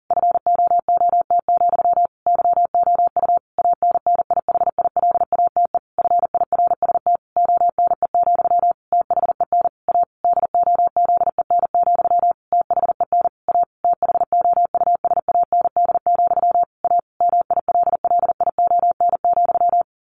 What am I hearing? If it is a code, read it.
Morse code, heard as a keyed tone, about 40 wpm, 700 Hz. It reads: FOOT, YOU ANNIHILATE FIRST ONE, THEN A DOZEN, THEN A THOUSAND, A MILLION,
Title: A Sound Of Thunder 40wpm